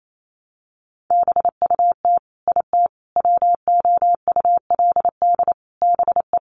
Morse code, heard as a keyed tone, about 28 wpm, 700 Hz. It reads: BUT IT WOULD BE
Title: Sentences from Top 600 Words 28wpm